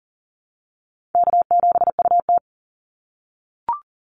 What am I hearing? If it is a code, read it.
Morse code, heard as a keyed tone, about 40 wpm, 700 Hz. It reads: K7UT
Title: Easy Call Signs - 40wpm